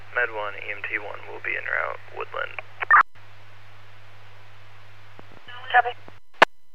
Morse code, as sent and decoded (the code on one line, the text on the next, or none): none